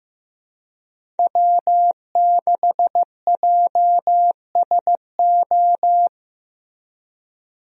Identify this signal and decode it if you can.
Morse code, heard as a keyed tone, about 15 wpm, 700 Hz. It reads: W6JSO
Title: USA Call Signs - 5 Characters - Repeated 3x - 15wpm